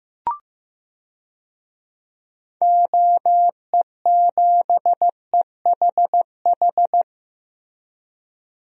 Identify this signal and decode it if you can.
Morse code, heard as a keyed tone, about 15 wpm, 700 Hz. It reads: OE7EHH
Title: DXCC-100 Call Signs - 6 Characters - Repeated 3x - 15wpm